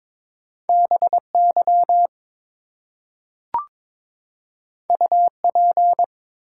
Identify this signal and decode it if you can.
Morse code, heard as a keyed tone, about 22 wpm, 700 Hz. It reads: BY UP
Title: Two Letter Words - 22wpm